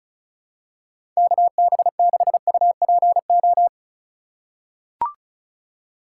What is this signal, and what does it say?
Morse code, heard as a keyed tone, about 36 wpm, 700 Hz. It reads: KB6UPO
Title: USA Call Signs - 6 Characters - 8x Character Spacing - 35wpm